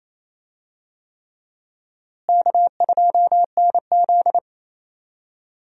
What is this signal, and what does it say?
Morse code, heard as a keyed tone, about 28 wpm, 700 Hz. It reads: K2NZ
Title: USA Call Signs - 4 Characters - Repeated 3x - 28wpm